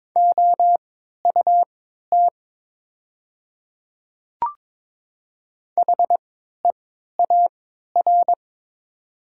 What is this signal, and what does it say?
Morse code, heard as a keyed tone, about 22 wpm, spaced as for 13 wpm, 700 Hz. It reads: OUT HEAR
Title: 3x Character Spacing - Top 300 Words - 22wpm